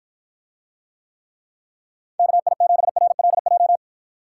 Morse code, heard as a keyed tone, about 53 wpm, 700 Hz. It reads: KI7RLJ